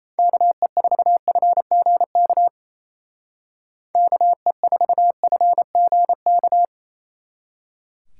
Morse code, heard as a keyed tone, about 30 wpm, 700 Hz. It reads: KE4FGK KE4FGK
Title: Speed Racing - USA 2x3 Call Signs 28wpm